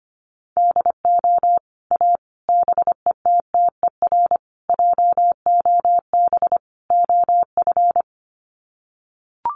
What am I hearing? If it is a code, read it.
Morse code, heard as a keyed tone, about 25 wpm, 700 Hz. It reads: DO A BETTER JOB OF
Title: Sets of 5 Words Encore2 25wpm